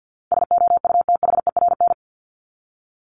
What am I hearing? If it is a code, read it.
Morse code, heard as a keyed tone, about 50 wpm, 700 Hz. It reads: SOUTHERN
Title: Top 1751-2000 Words 50wpm